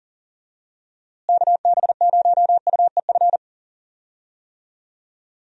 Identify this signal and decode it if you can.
Morse code, heard as a keyed tone, about 40 wpm, 700 Hz. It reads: KB0UEF
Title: USA Call Signs - 6 Characters - Repeated 2x - 40wpm